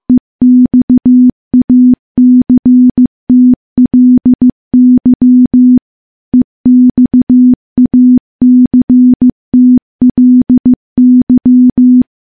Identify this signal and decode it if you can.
Morse code, heard as a keyed tone, about 15 wpm, 250 Hz. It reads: EXACTLY EXACTLY
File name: morse-training-15WPM